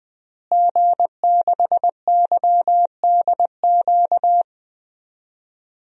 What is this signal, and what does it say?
Morse code, heard as a keyed tone, about 21 wpm, 700 Hz. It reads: G6YDQ
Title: DXCC-100 Call Signs - 5 Characters - Repeated 2x - 20wpm